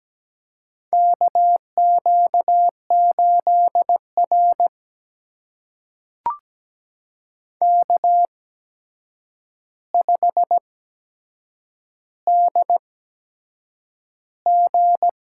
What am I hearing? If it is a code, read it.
Morse code, heard as a keyed tone, about 17 wpm, spaced as for 7 wpm, 700 Hz. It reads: KQ8R K5DG
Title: USA Call Signs - 4 Characters - 8x Character Spacing - 17wpm